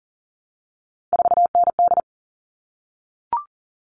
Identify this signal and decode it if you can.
Morse code, heard as a keyed tone, about 40 wpm, 700 Hz. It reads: 4ND
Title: All Cut Numbers - 3-Digit 40wpm